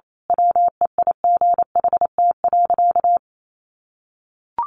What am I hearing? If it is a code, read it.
Morse code, heard as a keyed tone, about 28 wpm, 700 Hz. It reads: WEIGHT.
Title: A Sound Of Thunder 28wpm